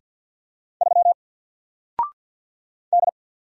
Morse code, heard as a keyed tone, about 50 wpm, 700 Hz. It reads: 3 D
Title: Single Letter-Number Alt 50wpm